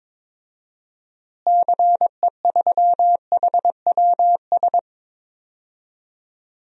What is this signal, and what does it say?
Morse code, heard as a keyed tone, about 22 wpm, 700 Hz. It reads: CE3HWS
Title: DXCC-100 Call Signs - 6 Characters - Repeated 3x - 22wpm